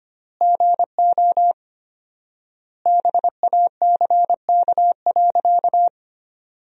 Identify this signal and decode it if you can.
Morse code, heard as a keyed tone, about 25 wpm, 700 Hz. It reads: GO BACK.
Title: 4x Word-Spacing - Sentences from Top 300 Words - 25wpm